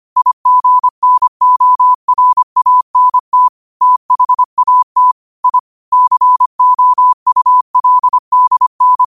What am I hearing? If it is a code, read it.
Morse code, heard as a keyed tone, about 25 wpm, 1000 Hz. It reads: IGNORANT THAT I COULDN
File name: qotdfiles / 25wpm